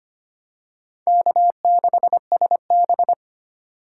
Morse code, heard as a keyed tone, about 25 wpm, 700 Hz. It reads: K6SB
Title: USA Call Signs - 4 Characters - 4x Character Spacing - 25wpm